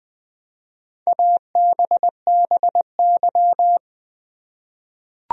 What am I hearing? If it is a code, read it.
Morse code, heard as a keyed tone, about 20 wpm, 700 Hz. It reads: ABBY
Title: USA Common Names - Max 4 Letters - 20wpm